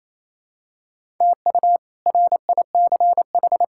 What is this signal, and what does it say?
Morse code, heard as a keyed tone, about 28 wpm, 700 Hz. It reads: TU RICH
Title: K1USN SST QSO Practice - 28wpm